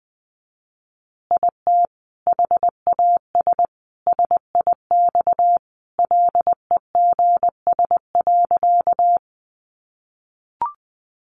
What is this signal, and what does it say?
Morse code, heard as a keyed tone, about 20 wpm, 700 Hz. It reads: IT HAS SIX LEGS.
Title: Additional Easy Phrases 20wpm